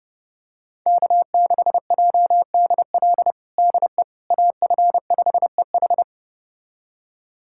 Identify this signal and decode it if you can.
Morse code, heard as a keyed tone, about 30 wpm, 700 Hz. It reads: K6JDL DE AF5EH
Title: USA Callsign DE Callsign - 5 Characters - 30wpm